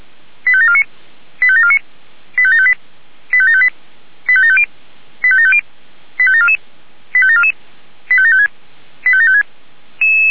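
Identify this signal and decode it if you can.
Five-tone selective call (ZVEI), heard as quick runs of five tones, tones of about 70 ms. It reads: 85749 85749 85758 85758 85750 85750 85744 85744 85756 85756
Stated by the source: Received by SDR, ZVEI five-tone.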